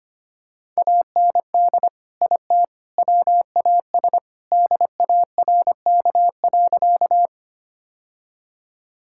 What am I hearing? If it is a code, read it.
Morse code, heard as a keyed tone, about 25 wpm, 700 Hz. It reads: AND IT WAS DARK.
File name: practice / 25wpm / Sentences with 3-Word Phrases - 25wpm